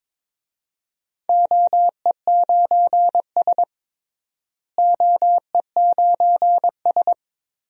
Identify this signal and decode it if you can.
Morse code, heard as a keyed tone, about 22 wpm, 700 Hz. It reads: OE9S OE9S
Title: DXCC-100 Call Signs - 4 Characters - Repeated 3x - 22wpm